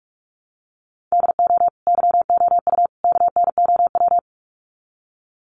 Morse code, heard as a keyed tone, about 44 wpm, 700 Hz. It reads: DO YOU KNOW
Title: Sets of 3 Words 45wpm